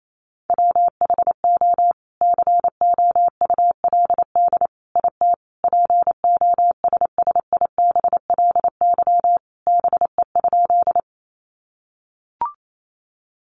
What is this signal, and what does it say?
Morse code, heard as a keyed tone, about 28 wpm, 700 Hz. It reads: WHO COULD IT POSSIBLY BE?
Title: Sentences from Top 1300 Words 28wpm